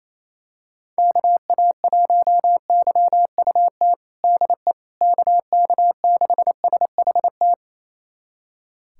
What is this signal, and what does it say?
Morse code, heard as a keyed tone, about 28 wpm, 700 Hz. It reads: KA1YUT DE KK6SHT
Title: USA Callsign DE Callsign 28wpm